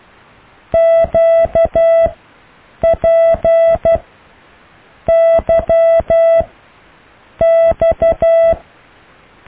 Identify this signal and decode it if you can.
Morse code, heard as a keyed tone, about 12 wpm, spaced as for 6 wpm, 660 Hz. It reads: QPYX
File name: cw / Alle letters